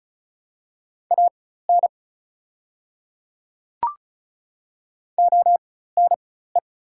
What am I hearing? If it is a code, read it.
Morse code, heard as a keyed tone, about 36 wpm, spaced as for 16 wpm, 690 Hz. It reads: AN ONE
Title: 4x Character Spacing - Top 100 Words - 35wpm